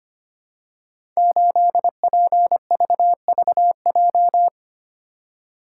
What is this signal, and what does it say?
Morse code, heard as a keyed tone, about 25 wpm, 700 Hz. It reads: 8PVVJ
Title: Code Groups - 5 Letter:Numbers - 25wpm